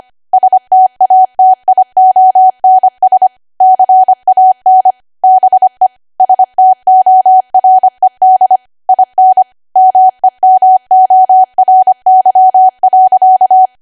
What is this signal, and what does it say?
Morse code, heard as a keyed tone, about 25 wpm, 750 Hz. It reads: STATIONS CAN BE STORED IN MEMORY.
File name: Archive / 25 WPM / 25WPM